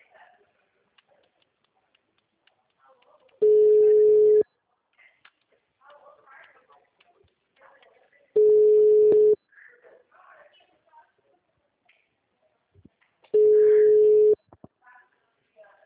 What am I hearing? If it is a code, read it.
Ringback tone (Europe, 425 Hz), with speech.